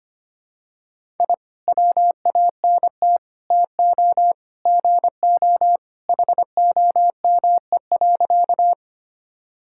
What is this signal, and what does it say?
Morse code, heard as a keyed tone, about 25 wpm, 700 Hz. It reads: I WANT TO GO HOME.
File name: practice / 25wpm / Sentences with 3-Word Phrases - 25wpm